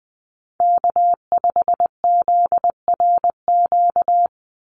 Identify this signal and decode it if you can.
Morse code, heard as a keyed tone, about 20 wpm, 700 Hz. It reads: K5ZRQ
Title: USA Call Signs 20wpm